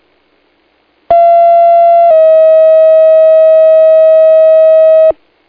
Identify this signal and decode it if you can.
Two-tone page, heard as a one-second tone, then a three-second tone, about 680 then 640 Hz. It is heard from 1.1 to 5.1 s.